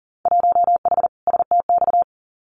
Morse code, heard as a keyed tone, about 41 wpm, 700 Hz. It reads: 1H STX
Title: Winter Field Day Exchanges 40wpm